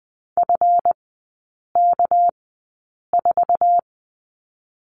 Morse code, heard as a keyed tone, about 20 wpm, spaced as for 8 wpm, 690 Hz. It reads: FK4
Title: 198 Lesson 32e - Letter k - Three Characters 20wpm